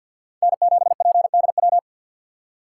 Morse code, heard as a keyed tone, about 53 wpm, 700 Hz. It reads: N7PDW
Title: USA Call Signs - 5 Characters - 2x Character Spacing - 50wpm